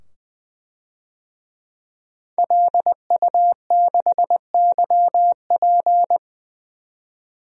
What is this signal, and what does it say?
Morse code, heard as a keyed tone, about 20 wpm, 700 Hz. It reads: LU6YP
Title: DXCC-100 Call Signs - 5 Characters - 20wpm